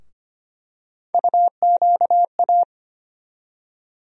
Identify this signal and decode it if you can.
Morse code, heard as a keyed tone, about 25 wpm, 700 Hz. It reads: UQA
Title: USA Call Sign Postfixes - 3 Letters 25wpm